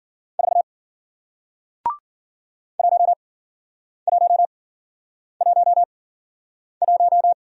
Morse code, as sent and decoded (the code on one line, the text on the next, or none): ....- / .---- .---- .---- .----
4 1111